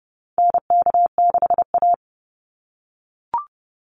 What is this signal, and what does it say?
Morse code, heard as a keyed tone, about 30 wpm, 700 Hz. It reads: NK6A
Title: NA SOTA Chaser Callsigns 30wpm